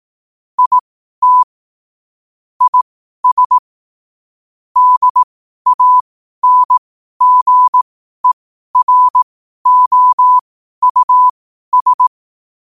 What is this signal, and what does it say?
Morse code, heard as a keyed tone, about 18 wpm, spaced as for 12 wpm, 1000 Hz. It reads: IT IS DANGEROUS
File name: qotdfiles / 12wpm